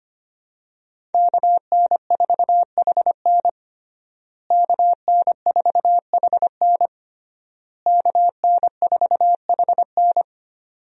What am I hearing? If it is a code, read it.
Morse code, heard as a keyed tone, about 25 wpm, 700 Hz. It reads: KN4HN KN4HN KN4HN
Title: USA Call Signs - 5 Characters - Repeated 3x - 25wpm